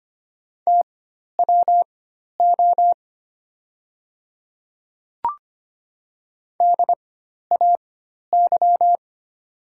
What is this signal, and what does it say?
Morse code, heard as a keyed tone, about 25 wpm, spaced as for 12 wpm, 700 Hz. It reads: TWO DAY
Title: 4x Character Spacing - Top 100 Words - 25wpm